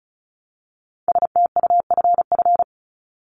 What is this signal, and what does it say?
Morse code, heard as a keyed tone, about 34 wpm, 700 Hz. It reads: STUFF
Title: Top 1251-1500 Words 35wpm